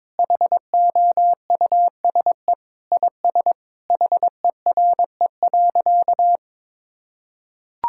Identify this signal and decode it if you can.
Morse code, heard as a keyed tone, about 21 wpm, 700 Hz. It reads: HOUSE IS HERE.
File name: practice / 22wpm / Sentences with 3-Word Phrases - 22wpm